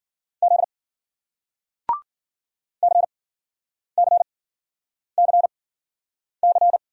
Morse code, heard as a keyed tone, about 54 wpm, spaced as for 9 wpm, 700 Hz. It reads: Z CCCC